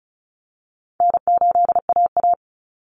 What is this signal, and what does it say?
Morse code, heard as a keyed tone, about 34 wpm, 700 Hz. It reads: N8AA
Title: Easy Call Signs Repeated 2x - 35wpm